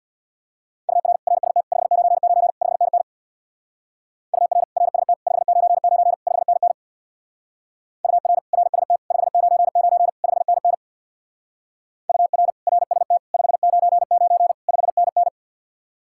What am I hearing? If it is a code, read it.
Morse code, heard as a keyed tone, about 66 wpm, 700 Hz. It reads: UR RST 599 5NN UR RST 599 5NN UR RST 599 5NN UR RST 599 5NN